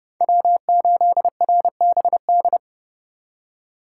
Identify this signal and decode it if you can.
Morse code, heard as a keyed tone, about 30 wpm, 700 Hz. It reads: W8RBD
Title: USA Call Signs - 5 Characters - 2x Character Spacing - 30wpm